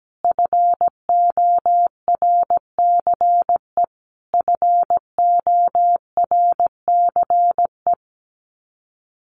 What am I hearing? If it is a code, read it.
Morse code, heard as a keyed tone, about 17 wpm, 700 Hz. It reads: FORCE FORCE
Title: Top 500 Words Repeated 17wpm